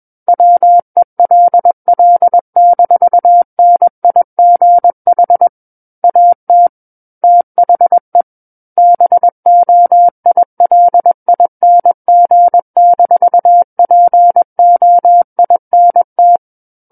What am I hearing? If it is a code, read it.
Morse code, heard as a keyed tone, about 21 wpm, 700 Hz. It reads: WELL-NIGH AT THE BOILING-POINT
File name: Book One / 12-Chapter 12 - 21 WPM